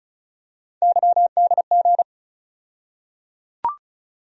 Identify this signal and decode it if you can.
Morse code, heard as a keyed tone, about 36 wpm, 700 Hz. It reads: YDG
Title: USA Call Sign Postfixes - 3 Letters 35wpm